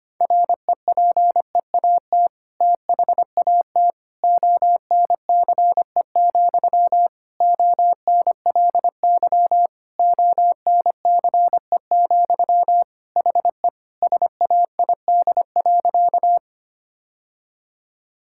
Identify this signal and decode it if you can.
Morse code, heard as a keyed tone, about 25 wpm, 700 Hz. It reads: REPEAT THAT ONCE, ONLY ONCE, HE SAID.
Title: Sentences from Top 1600 Words 25wpm